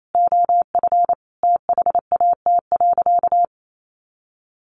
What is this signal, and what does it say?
Morse code, heard as a keyed tone, about 28 wpm, 700 Hz. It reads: OF THAT.
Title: Sentences from Top 300 Words 28wpm